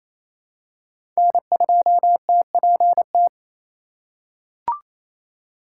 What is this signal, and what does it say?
Morse code, heard as a keyed tone, about 28 wpm, 700 Hz. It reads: N2TPT
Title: USA Call Signs - 5 Characters - 4x Character Spacing - 28wpm